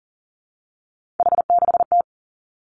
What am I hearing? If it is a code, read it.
Morse code, heard as a keyed tone, about 37 wpm, 700 Hz. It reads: H6T
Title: Three Challenging Letter-Number Combinations 40wpm